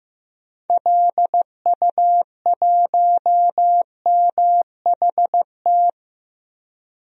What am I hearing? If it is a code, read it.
Morse code, heard as a keyed tone, about 15 wpm, 700 Hz. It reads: LU1MHT
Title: DXCC-100 Call Signs - 6 Characters - 15wpm